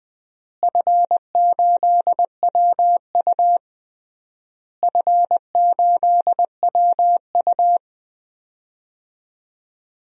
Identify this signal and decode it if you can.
Morse code, heard as a keyed tone, about 20 wpm, 700 Hz. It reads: F8WU F8WU
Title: DXCC-100 Call Signs - 4 Characters - Repeated 3x - 20wpm